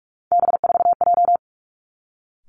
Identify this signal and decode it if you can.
Morse code, heard as a keyed tone, about 46 wpm, 700 Hz. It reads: B4J
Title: Three Challenging Letter-Number Combinations 45wpm